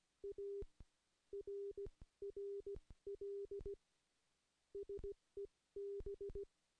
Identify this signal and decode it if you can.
Morse code, heard as a keyed tone, about 17 wpm, spaced as for 13 wpm, 400 Hz. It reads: ARRL SEB
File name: contests / CW-SS-k5tr- -0700